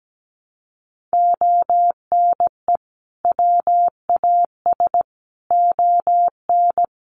Morse code, heard as a keyed tone, about 17 wpm, 700 Hz. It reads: ONE WAS ON
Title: Sentences from Top 300 Words 17wpm